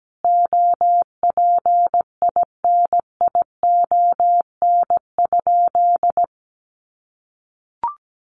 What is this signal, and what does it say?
Morse code, heard as a keyed tone, about 17 wpm, 700 Hz. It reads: OPINION?
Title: Sentences from Top 1200 Words 17wpm